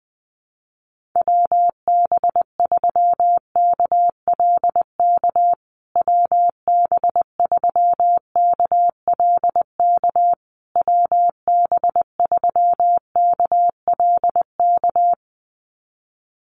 Morse code, heard as a keyed tone, about 20 wpm, 700 Hz. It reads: WB3KLK WB3KLK WB3KLK
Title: USA Call Signs Repeated 3x 20wpm